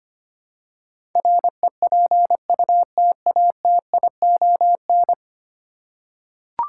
Morse code, heard as a keyed tone, about 25 wpm, 700 Hz. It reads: REPUTATION
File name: 10 Letter Words - First 3 Letters Given - 25wpm